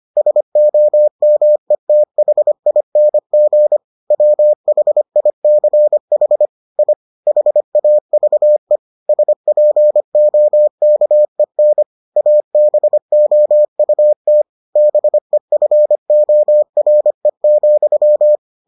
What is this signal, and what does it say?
Morse code, heard as a keyed tone, about 25 wpm, 600 Hz. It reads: SOMETHING WHICH I HAVE SPOKEN ABOUT BEFORE,